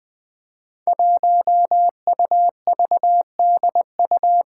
Morse code, heard as a keyed tone, about 20 wpm, 700 Hz. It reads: 1UVDU
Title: Five Challenging Letter-Number Combinations 20wpm